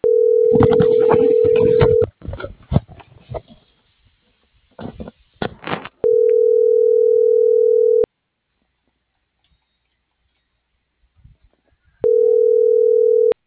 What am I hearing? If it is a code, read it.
Ringback tone (North America), 440 + 480 Hz.